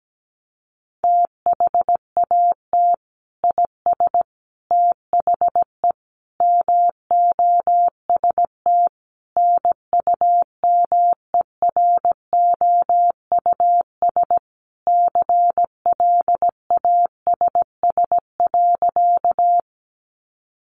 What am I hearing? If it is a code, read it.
Morse code, heard as a keyed tone, about 17 wpm, 700 Hz. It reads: THAT IS THE MOST NUMEROUS CLASS.
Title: Sentences from Top 1400 Words 17wpm